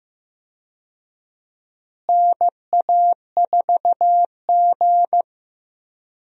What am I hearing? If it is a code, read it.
Morse code, heard as a keyed tone, about 15 wpm, 700 Hz. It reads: NA4G
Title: USA Call Signs - 4 Characters - Repeated 3x - 15wpm